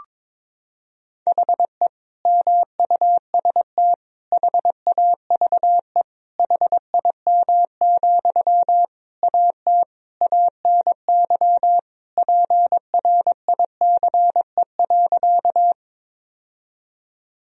Morse code, heard as a keyed tone, about 22 wpm, 700 Hz. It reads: HE MUST HAVE HIM, AT ANY PRICE.
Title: Sentences from Top 1000 Words - Review Part 4 - 22wpm